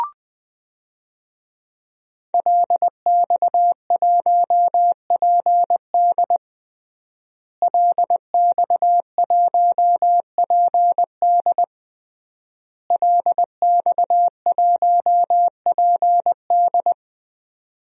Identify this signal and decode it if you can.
Morse code, heard as a keyed tone, about 21 wpm, 700 Hz. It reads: LX1PD LX1PD LX1PD
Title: DXCC-100 Call Signs - 5 Characters - Repeated 3x - 20wpm